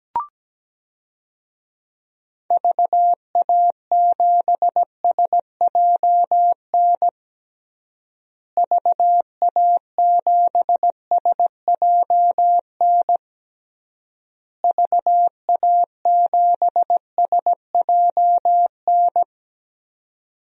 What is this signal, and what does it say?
Morse code, heard as a keyed tone, about 17 wpm, 700 Hz. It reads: VA7SJN VA7SJN VA7SJN
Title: DXCC-100 Call Signs - 6 Characters - Repeated 3x - 17wpm